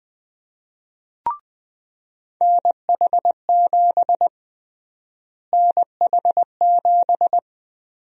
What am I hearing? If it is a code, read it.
Morse code, heard as a keyed tone, about 20 wpm, 700 Hz. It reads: NH7 NH7
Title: USA Call Sign Prefixes - Repeated 2x - 20wpm